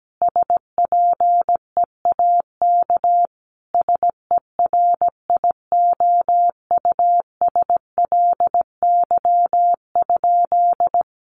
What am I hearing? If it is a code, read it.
Morse code, heard as a keyed tone, about 17 wpm, 700 Hz. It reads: SPEAK SERIOUSLY?
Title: Sentences from Top 1200 Words 17wpm